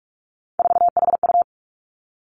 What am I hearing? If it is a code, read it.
Morse code, heard as a keyed tone, about 46 wpm, 700 Hz. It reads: 4HU
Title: Three Challenging Letter-Number Combinations 45wpm